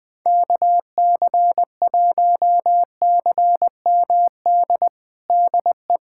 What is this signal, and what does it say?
Morse code, heard as a keyed tone, about 21 wpm, 700 Hz. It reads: KC1CMD DE
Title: USA Callsign DE Callsign 20wpm